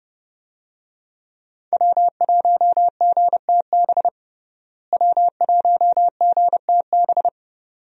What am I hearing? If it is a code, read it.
Morse code, heard as a keyed tone, about 28 wpm, 700 Hz. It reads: W1GTB W1GTB
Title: USA Call Signs - 5 Characters - Repeated 2x - 30wpm